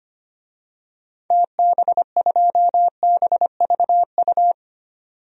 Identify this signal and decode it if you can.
Morse code, heard as a keyed tone, about 25 wpm, 700 Hz. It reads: TB2BVU
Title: DXCC-100 Call Signs - 6 Characters - 25wpm